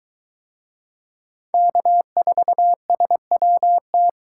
Morse code, heard as a keyed tone, about 23 wpm, 700 Hz. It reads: K4SWT